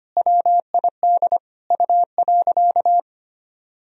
Morse code, heard as a keyed tone, about 25 wpm, 700 Hz. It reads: WID U.
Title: Sentences from Top 300 Words and Abbreviations 25wpm